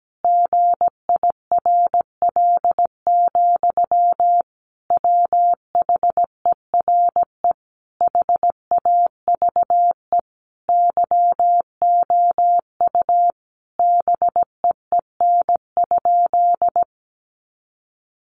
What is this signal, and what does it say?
Morse code, heard as a keyed tone, about 17 wpm, 700 Hz. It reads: GIRL, WHERE HAVE YOU BEEN?
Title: Sentences from Top 400 Words 17wpm